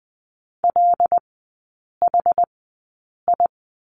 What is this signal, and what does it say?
Morse code, heard as a keyed tone, about 20 wpm, spaced as for 8 wpm, 700 Hz. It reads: LHI
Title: 100 Lesson 17e - Letter u - Three Characters 20wpm